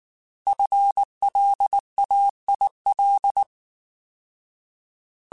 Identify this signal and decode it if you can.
Morse code, heard as a keyed tone, about 19 wpm, 800 Hz. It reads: FLAIL
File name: Alphabetical / Alphabetical mp3s / Fs @18